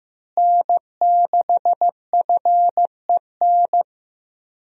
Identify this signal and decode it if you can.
Morse code, heard as a keyed tone, about 15 wpm, 700 Hz. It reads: N6FEN
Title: USA Call Signs - 5 Characters - Repeated 3x - 15wpm